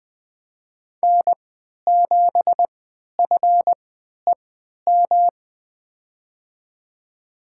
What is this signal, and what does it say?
Morse code, heard as a keyed tone, about 20 wpm, spaced as for 11 wpm, 700 Hz. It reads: N7FEM
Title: USA Call Signs - 5 Characters - 3x Character Spacing - 20wpm